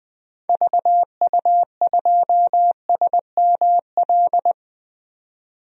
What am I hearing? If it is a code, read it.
Morse code, heard as a keyed tone, about 20 wpm, 700 Hz. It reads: VU2SML
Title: DXCC-100 Call Signs - 6 Characters - 20wpm